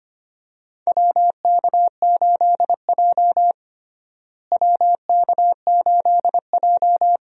Morse code, heard as a keyed tone, about 25 wpm, 700 Hz. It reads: WK8J WK8J
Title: USA Call Signs - 4 Characters - Repeated 2x - 25wpm